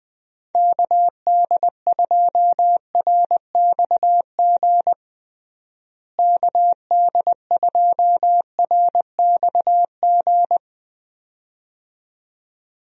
Morse code, heard as a keyed tone, about 20 wpm, 700 Hz. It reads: KD2RXG KD2RXG
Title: USA Call Signs - 6 Characters - Repeated 2x - 20wpm